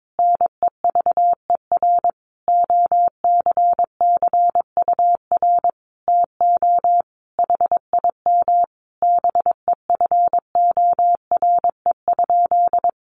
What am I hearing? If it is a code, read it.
Morse code, heard as a keyed tone, about 22 wpm, 700 Hz. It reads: NEVER OCCUR TO HIM BEFORE?
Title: Sentences from Top 900 Words 22wpm